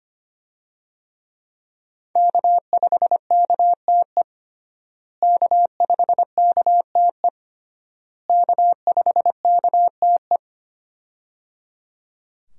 Morse code, heard as a keyed tone, about 25 wpm, 700 Hz. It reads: K5KTE K5KTE K5KTE
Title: USA Call Signs - 5 Characters - Repeated 3x - 25wpm